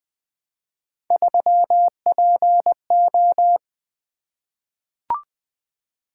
Morse code, heard as a keyed tone, about 21 wpm, 700 Hz. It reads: 3PO
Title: Three Letter-Number Combinations - 20wpm